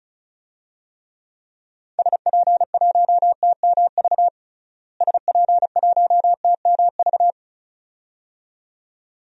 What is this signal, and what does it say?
Morse code, heard as a keyed tone, about 34 wpm, 700 Hz. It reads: SP1TMV SP1TMV
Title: DXCC-100 Call Signs - 6 Characters - Repeated 2x - 35wpm